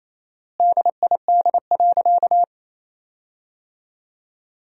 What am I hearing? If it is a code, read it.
Morse code, heard as a keyed tone, about 28 wpm, 700 Hz. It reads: DID.
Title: 4x Word-Spacing - Sentences from Top 100 Words - 28wpm